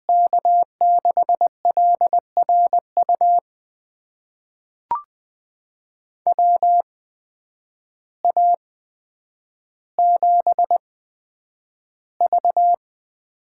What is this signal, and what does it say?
Morse code, heard as a keyed tone, about 21 wpm, spaced as for 9 wpm, 700 Hz. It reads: K6LRU WA7V
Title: USA Call Signs - 6 Characters - 8x Character Spacing - 20wpm